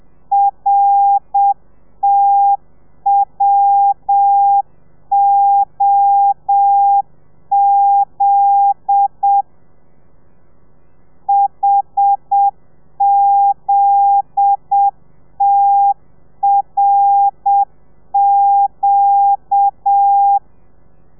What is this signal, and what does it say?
Morse code, heard as a keyed tone, about 7 wpm, 800 Hz. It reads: RTWOZ HZTRQ